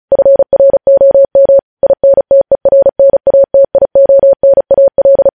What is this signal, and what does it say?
Morse code, heard as a keyed tone, about 36 wpm, 550 Hz. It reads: FROM INTERNATIONAL
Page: news headlines in morse code at 35 wpm